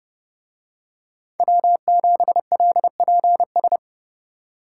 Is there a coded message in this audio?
Yes: Morse code from 1.4 to 3.8 s, heard as a keyed tone, about 30 wpm, 700 Hz. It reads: W7LPS